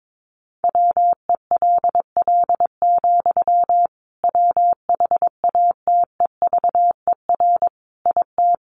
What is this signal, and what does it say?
Morse code, heard as a keyed tone, about 22 wpm, 700 Hz. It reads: WELL, WHATEVER IT
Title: Sentences from Top 1100 Words 22wpm